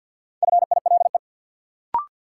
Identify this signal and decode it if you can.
Morse code, heard as a keyed tone, about 52 wpm, 700 Hz. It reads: FILE
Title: Sentences from Top 1200 Words - New Word Drill Alt - 50wpm